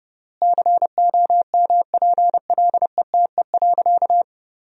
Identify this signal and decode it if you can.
Morse code, heard as a keyed tone, about 32 wpm, 700 Hz. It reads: COMPLETE.
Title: 3x Word-Spacing - Sentences from Top 400 Words - 30wpm